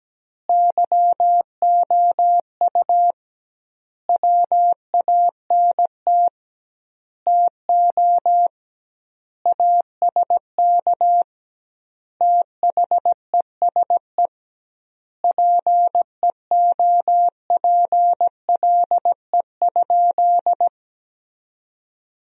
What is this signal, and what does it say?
Morse code, heard as a keyed tone, about 17 wpm, 700 Hz. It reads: YOU WANT TO ASK THESE PEOPLE?
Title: 2x Word-Spacing - Sentences from Top 200 Words - 17wpm